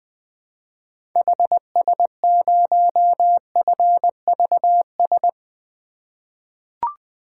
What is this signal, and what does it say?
Morse code, heard as a keyed tone, about 20 wpm, 700 Hz. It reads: HS0FVS
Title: 2023 POTA Activator Call Signs - 20wpm